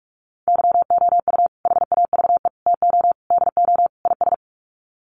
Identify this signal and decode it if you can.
Morse code, heard as a keyed tone, about 44 wpm, 700 Hz. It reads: YOU HAVE TO DO IS
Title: Sets of 5 Words 45wpm